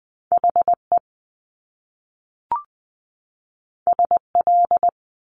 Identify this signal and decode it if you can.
Morse code, heard as a keyed tone, about 20 wpm, 700 Hz. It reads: HE SL
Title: Top Two Letter Combinations in Words 20wpm